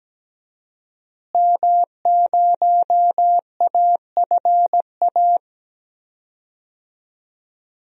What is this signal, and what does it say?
Morse code, heard as a keyed tone, about 17 wpm, 700 Hz. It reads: M0AFA
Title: DXCC-100 Call Signs - 5 Characters - Repeated 2x - 17wpm